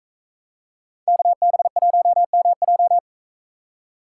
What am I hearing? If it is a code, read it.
Morse code, heard as a keyed tone, about 40 wpm, 700 Hz. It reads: KB1MJ